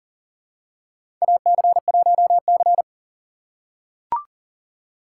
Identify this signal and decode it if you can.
Morse code, heard as a keyed tone, about 40 wpm, 700 Hz. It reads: AC1C
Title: USA Call Signs - 4 Characters - 4x Character Spacing - 40wpm